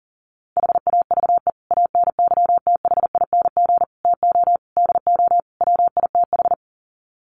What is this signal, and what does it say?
Morse code, heard as a keyed tone, about 40 wpm, 700 Hz. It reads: HAVE ANYTHING TO DO WITH
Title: Sets of 5 Words 40wpm